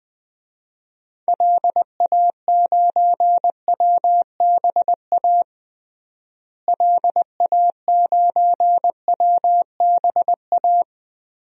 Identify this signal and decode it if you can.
Morse code, heard as a keyed tone, about 20 wpm, 700 Hz. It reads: LA9WBA LA9WBA
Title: DXCC-100 Call Signs - 6 Characters - Repeated 3x - 20wpm